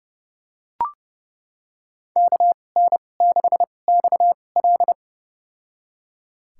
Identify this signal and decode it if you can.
Morse code, heard as a keyed tone, about 29 wpm, spaced as for 21 wpm, 690 Hz. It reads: KN6XL
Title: USA Call Signs - 5 Characters - 2x Character Spacing - 30wpm